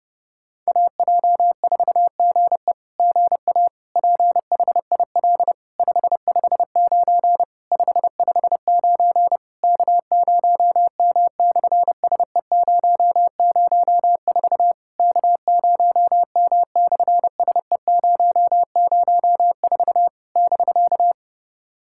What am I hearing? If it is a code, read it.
Morse code, heard as a keyed tone, about 30 wpm, 700 Hz. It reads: AJ4GE GA PHIL 559 559 K0M/SE004 K0M/SE004 <BK>
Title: SOTA QSO Practice 30wpm